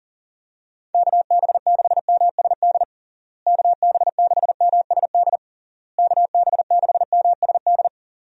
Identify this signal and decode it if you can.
Morse code, heard as a keyed tone, about 40 wpm, 700 Hz. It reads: KB6MSD KB6MSD KB6MSD
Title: USA Call Signs - 6 Characters - Repeated 3x - 40wpm